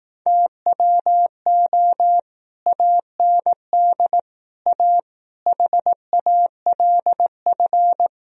Morse code, heard as a keyed tone, about 18 wpm, 700 Hz. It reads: TWO AND A HALF
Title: Speed-Racing - Sets of 5 Words 15wpm